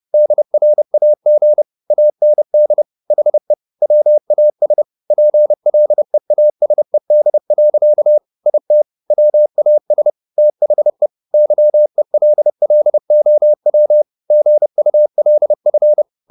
Morse code, heard as a keyed tone, about 30 wpm, 600 Hz. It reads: DRAG AND HE WAS PLEASED. IT WAS THE YELLOW GULF
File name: the old man and the sea / the old man and the sea 1 30wpm